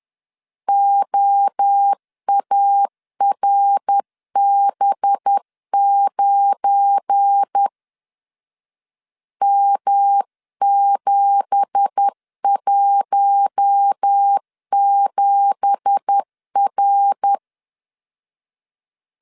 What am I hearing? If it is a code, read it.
Morse code, heard as a keyed tone, about 10 wpm, 800 Hz. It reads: OARB9 M717R
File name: CW / DET / 1 9 1500C S A 10wpm 4511